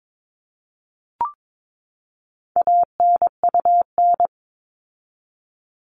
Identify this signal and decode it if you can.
Morse code, heard as a keyed tone, about 22 wpm, 700 Hz. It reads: ANUN
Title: All Cut Numbers - 4-Digit 22wpm